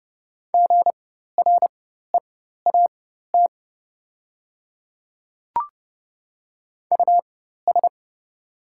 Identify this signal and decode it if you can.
Morse code, heard as a keyed tone, about 29 wpm, spaced as for 14 wpm, 700 Hz. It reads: GREAT US
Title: 4x Character Spacing - Top 300 Words - 30wpm